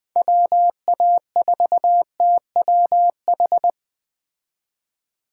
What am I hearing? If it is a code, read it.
Morse code, heard as a keyed tone, about 21 wpm, 700 Hz. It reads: WA4TWH